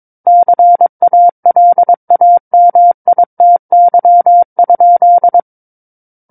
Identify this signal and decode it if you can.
Morse code, heard as a keyed tone, about 22 wpm, 700 Hz. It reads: CALAMITY?
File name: Book One / 13-Chapter 13 - 22 WPM